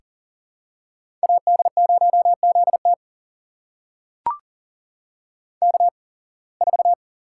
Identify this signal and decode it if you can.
Morse code, heard as a keyed tone, about 40 wpm, spaced as for 26 wpm, 700 Hz. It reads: AD0ZT K4
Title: USA Call Signs - 8x Character Spacing - 40wpm